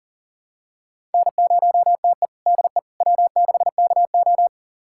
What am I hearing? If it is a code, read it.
Morse code, heard as a keyed tone, about 40 wpm, 700 Hz. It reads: N0TE DE W6KO
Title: USA Callsign DE Callsign - 4 Characters - 40wpm